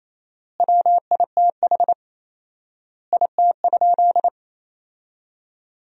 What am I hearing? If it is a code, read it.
Morse code, heard as a keyed tone, about 28 wpm, 700 Hz. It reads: WITH IT?
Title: 4x Word-Spacing - Sentences from Top 100 Words - 28wpm